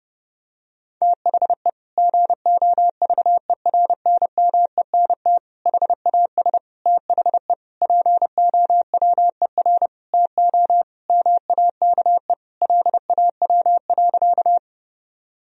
Morse code, heard as a keyed tone, about 32 wpm, 700 Hz. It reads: THE GOVERNMENT HAS THE POWER TO MAKE LAW.
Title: Sentences from Top 1000 Words - Review Part 3 - 30wpm